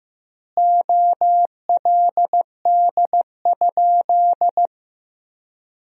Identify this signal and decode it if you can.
Morse code, heard as a keyed tone, about 15 wpm, 700 Hz. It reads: OLD?
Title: 3x Word-Spacing - Sentences from Top 400 Words - 15wpm